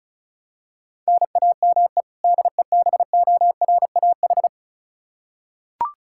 Morse code, heard as a keyed tone, about 34 wpm, 700 Hz. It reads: NAME DEBORAH
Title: Practice QSO - Exchange 1 - Beginning - 35wpm